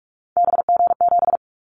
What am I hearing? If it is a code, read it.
Morse code, heard as a keyed tone, about 46 wpm, 700 Hz. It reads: BG7
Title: Callsign Prefixes for Top 100 DXCC - 45wpm